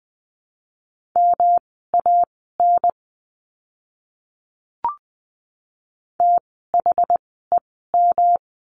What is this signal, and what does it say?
Morse code, heard as a keyed tone, about 20 wpm, spaced as for 14 wpm, 700 Hz. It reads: MAN THEM
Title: 244 Lesson 39f - Letter z - Words 20wpm